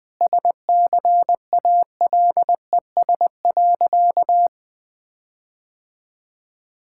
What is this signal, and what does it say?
Morse code, heard as a keyed tone, about 21 wpm, 700 Hz. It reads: SCALES.
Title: All-Age Appropriate Jokes - 20wpm